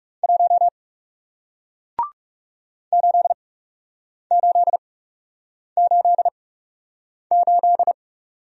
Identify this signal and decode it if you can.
Morse code, heard as a keyed tone, about 40 wpm, spaced as for 8 wpm, 700 Hz. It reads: 1 8888